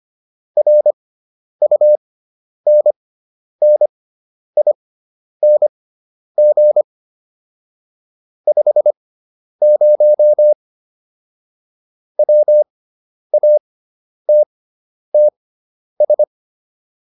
Morse code, heard as a keyed tone, about 25 wpm, spaced as for 10 wpm, 600 Hz. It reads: RUNNING 50 WATTS